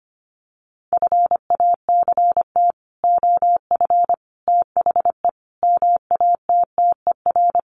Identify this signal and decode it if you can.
Morse code, heard as a keyed tone, about 25 wpm, 700 Hz. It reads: FACT OF THE MATTER
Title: Sets of 5 Words 25wpm